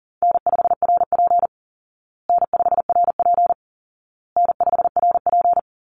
Morse code, heard as a keyed tone, about 40 wpm, 700 Hz. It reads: N5RP N5RP N5RP
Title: Easy Call Signs Repeated 3x - 40wpm